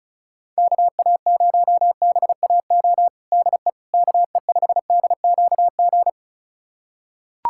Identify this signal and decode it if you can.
Morse code, heard as a keyed tone, about 34 wpm, 700 Hz. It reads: KA0BAO DE KE5DQG
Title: USA Callsign DE Callsign 35wpm